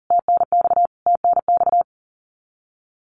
Morse code, heard as a keyed tone, about 40 wpm, 700 Hz. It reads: TNX TNX
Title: Abbreviations Repeated 3x - 40wpm